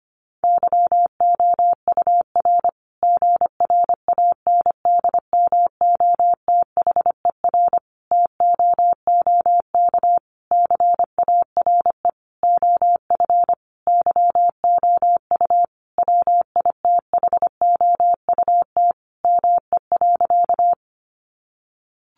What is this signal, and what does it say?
Morse code, heard as a keyed tone, about 25 wpm, 700 Hz. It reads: YOUR GRANDMOTHER TOOK CARE OF YOU WITHOUT ME.
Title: Sentences from Top 1200 Words 25wpm